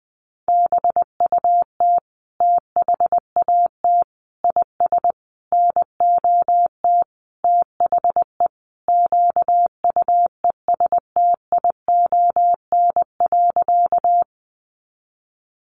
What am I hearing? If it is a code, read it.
Morse code, heard as a keyed tone, about 20 wpm, 700 Hz. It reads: BUT THAT IS NOT THE QUESTION.
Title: Sentences from Top 600 Words 20wpm